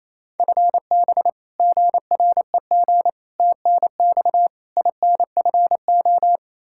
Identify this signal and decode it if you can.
Morse code, heard as a keyed tone, about 27 wpm, 700 Hz. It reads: FB GREG TNX INFO
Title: Practice QSO - Exchange 3 - More Info - 28wpm